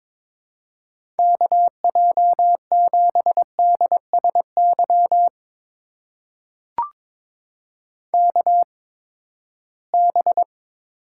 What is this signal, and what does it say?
Morse code, heard as a keyed tone, about 22 wpm, spaced as for 15 wpm, 700 Hz. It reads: KJ7DSY KB
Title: USA Call Signs - 6 Characters - 8x Character Spacing - 22wpm